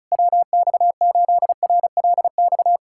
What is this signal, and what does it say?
Morse code, heard as a keyed tone, about 36 wpm, 700 Hz. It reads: WX8RLX
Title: USA Callsign DE Callsign 35wpm